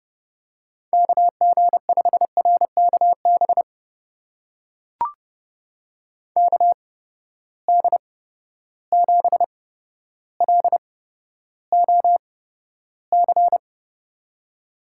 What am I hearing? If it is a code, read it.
Morse code, heard as a keyed tone, about 30 wpm, spaced as for 13 wpm, 700 Hz. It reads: KG5RKB KD7LOC
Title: USA Call Signs - 6 Characters - 8x Character Spacing - 30wpm